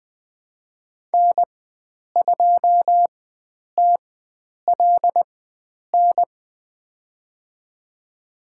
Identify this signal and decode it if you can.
Morse code, heard as a keyed tone, about 20 wpm, spaced as for 9 wpm, 700 Hz. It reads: N2TLN
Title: USA Call Signs - 5 Characters - 4x Character Spacing - 20wpm